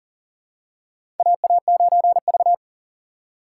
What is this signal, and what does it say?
Morse code, heard as a keyed tone, about 40 wpm, 700 Hz. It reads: AA9V
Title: USA Call Signs - 4 Characters - 2x Character Spacing - 40wpm